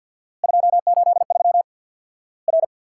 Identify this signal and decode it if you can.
Morse code, heard as a keyed tone, about 50 wpm, 700 Hz. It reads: 283 R
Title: Worked All Europe DX Contest - QTC Exchange with Repeats - 50wpm